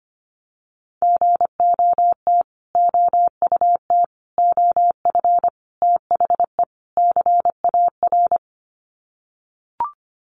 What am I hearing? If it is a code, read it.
Morse code, heard as a keyed tone, about 25 wpm, 700 Hz. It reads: GOT OUT OF THE CAR
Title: Sets of 5 Words Encore 25wpm